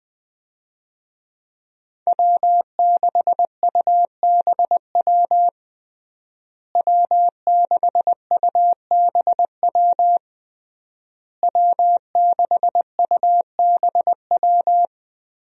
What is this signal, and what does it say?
Morse code, heard as a keyed tone, about 20 wpm, 700 Hz. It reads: W6UBW W6UBW W6UBW
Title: USA Call Signs - 5 Characters - Repeated 3x - 20wpm